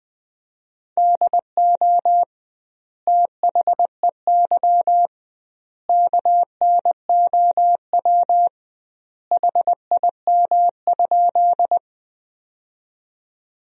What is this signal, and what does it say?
Morse code, heard as a keyed tone, about 20 wpm, 700 Hz. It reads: DO THEY KNOW HIM?
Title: 2x Word-Spacing - Sentences from Top 100 Words - 20wpm